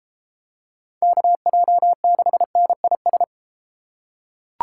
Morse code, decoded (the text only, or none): KJ6NIS